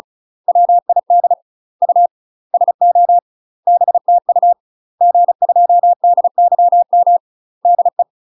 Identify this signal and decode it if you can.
Morse code, heard as a keyed tone, about 35 wpm, 700 Hz. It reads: WID U SO BTU G2DYM DE
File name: UK MASTER QSOs TRAINING / 35WPM / UK MASTER QSOs 0018